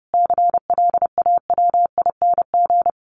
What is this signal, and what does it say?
Morse code, heard as a keyed tone, about 30 wpm, 700 Hz. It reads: CLAWING
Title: A Sound Of Thunder 30wpm